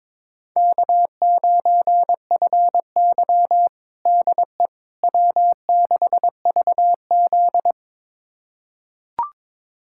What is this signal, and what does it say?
Morse code, heard as a keyed tone, about 22 wpm, 700 Hz. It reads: K9FY DE W6VZ
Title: USA Callsign DE Callsign - 4 Characters - 22wpm